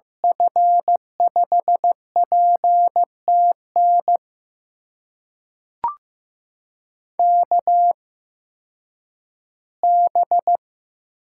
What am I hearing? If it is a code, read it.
Morse code, heard as a keyed tone, about 15 wpm, spaced as for 10 wpm, 700 Hz. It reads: F5PTN KB
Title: USA Call Signs - 6 Characters - 8x Character Spacing - 15wpm